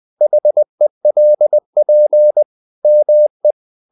Morse code, heard as a keyed tone, about 20 wpm, 600 Hz. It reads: HELP ME
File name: the old man and the sea / the old man and the sea part 1 20wpm